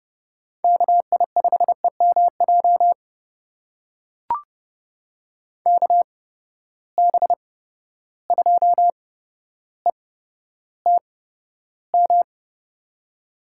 Morse code, heard as a keyed tone, about 30 wpm, spaced as for 13 wpm, 700 Hz. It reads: KI5EMJ KB2ETM
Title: USA Call Signs - 6 Characters - 8x Character Spacing - 30wpm